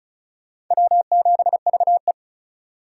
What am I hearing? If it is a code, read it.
Morse code, heard as a keyed tone, about 42 wpm, 700 Hz. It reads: W7VE
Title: Speed Racing - USA 1x2 Call Signs 35wpm